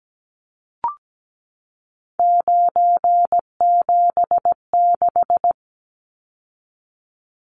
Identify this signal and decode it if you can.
Morse code, heard as a keyed tone, about 17 wpm, 700 Hz. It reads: 976
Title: Numbers - 3-Digit 17wpm